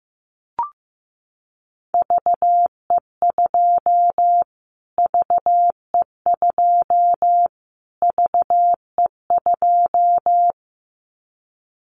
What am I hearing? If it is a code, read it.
Morse code, heard as a keyed tone, about 15 wpm, 700 Hz. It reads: VE2 VE2 VE2
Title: Callsign Prefixes for Top 25 DXCC - Repeated 3x - 15wpm